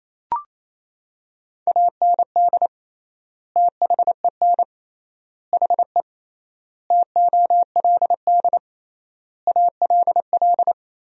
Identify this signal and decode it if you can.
Morse code, heard as a keyed tone, about 28 wpm, 700 Hz. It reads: AND THEN HE TOLD ALL
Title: 3x Word-Spacing - Sentences from Top 400 Words - 28wpm